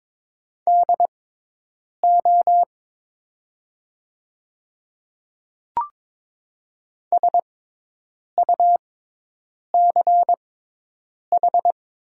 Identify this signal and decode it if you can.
Morse code, heard as a keyed tone, about 22 wpm, spaced as for 8 wpm, 700 Hz. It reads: DO SUCH
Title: 6x Character Spacing - Top 200 Words - 22wpm